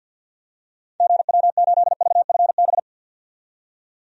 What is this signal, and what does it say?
Morse code, heard as a keyed tone, about 52 wpm, 700 Hz. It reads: GW8VFB
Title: DXCC-100 Call Signs - 6 Characters - 50wpm